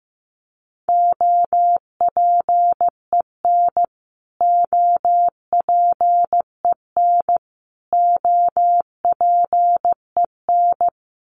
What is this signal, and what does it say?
Morse code, heard as a keyed tone, about 15 wpm, 700 Hz. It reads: OPEN OPEN OPEN
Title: Top 500 Words Repeated 15wpm